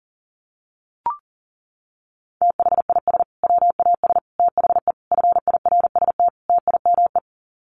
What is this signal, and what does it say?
Morse code, heard as a keyed tone, about 40 wpm, 700 Hz. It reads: THIS WAS THE FIRST TIME
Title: Sets of 5 Words Encore 40wpm